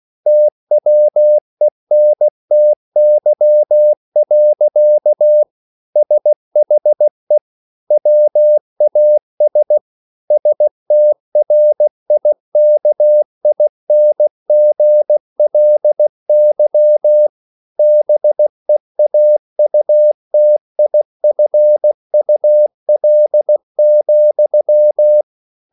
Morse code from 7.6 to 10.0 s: WAS

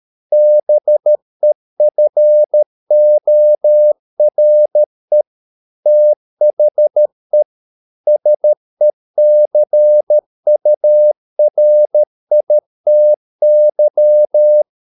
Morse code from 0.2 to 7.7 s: BEFORE THE

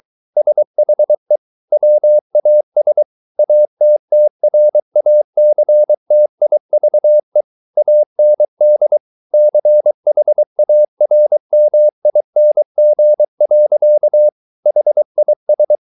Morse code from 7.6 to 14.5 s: AND CHARMING.

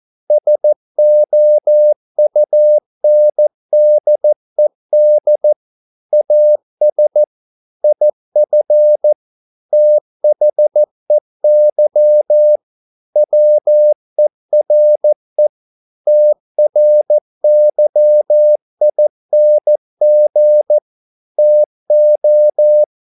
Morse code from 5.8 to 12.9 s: AS IF THEY

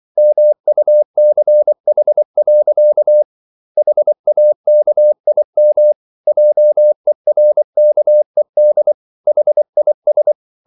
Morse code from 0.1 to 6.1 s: MUCH. HAKIM

